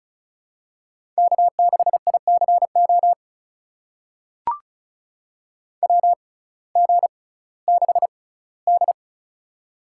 Morse code, heard as a keyed tone, about 36 wpm, spaced as for 19 wpm, 700 Hz. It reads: K6ICO WG6D
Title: USA Call Signs - 6x Character Spacing - 35wpm